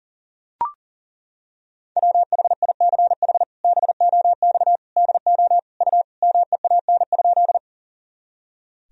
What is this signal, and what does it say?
Morse code, heard as a keyed tone, about 40 wpm, 700 Hz. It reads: WHICH BOX DO U MEAN?
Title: Sentences from Top 500 Words and Abbreviations 40wpm